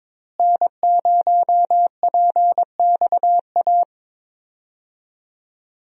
Morse code, heard as a keyed tone, about 22 wpm, 700 Hz. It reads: N0PXA
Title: USA Call Signs - 5 Characters - Repeated 3x - 22wpm